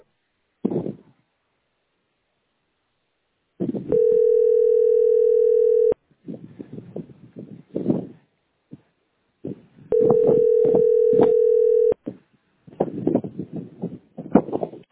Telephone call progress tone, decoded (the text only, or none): ringback tone